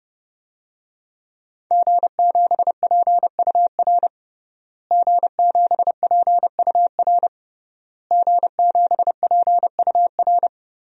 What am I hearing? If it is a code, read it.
Morse code, heard as a keyed tone, about 32 wpm, 700 Hz. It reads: G7PUR G7PUR G7PUR
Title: DXCC-100 Call Signs - 5 Characters - Repeated 3x - 30wpm